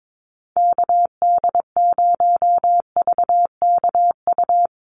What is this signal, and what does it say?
Morse code, heard as a keyed tone, about 22 wpm, 700 Hz. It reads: KD0VKU
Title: USA Call Signs Repeated 2x 22wpm